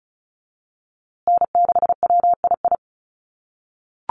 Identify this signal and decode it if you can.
Morse code, heard as a keyed tone, about 34 wpm, 700 Hz. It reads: N6WII
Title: USA Call Signs Repeated 3x 35wpm